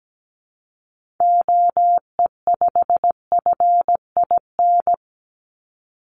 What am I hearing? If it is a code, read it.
Morse code, heard as a keyed tone, about 17 wpm, 700 Hz. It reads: OE5FIN
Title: Call Signs - 17wpm